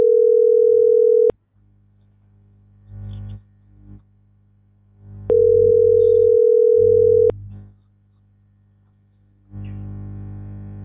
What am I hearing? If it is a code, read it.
Ringback tone (North America), 440 + 480 Hz.